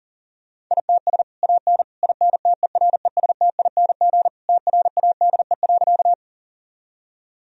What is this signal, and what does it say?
Morse code, heard as a keyed tone, about 40 wpm, 700 Hz. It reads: ITS AN INTERESTING TRADE.
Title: Sentences from Top 1000 Words - Review Part 4 - 40wpm